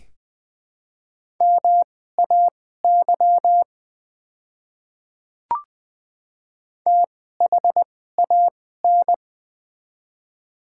Morse code, heard as a keyed tone, about 20 wpm, spaced as for 15 wpm, 700 Hz. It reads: MAY THAN
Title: 2x Character Spacing - Top 100 Words - 20wpm